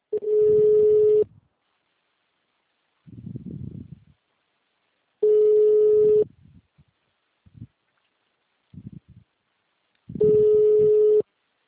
Ringback tone (Europe, 425 Hz).
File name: out- -2011- - - .44475